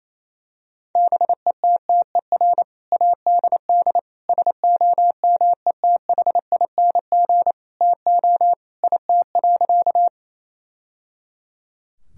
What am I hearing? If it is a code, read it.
Morse code, heard as a keyed tone, about 28 wpm, 700 Hz. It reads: BETTER ADD SOMETHING TO IT.
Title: Sentences from Top 1000 Words - Review Part 1 - 28wpm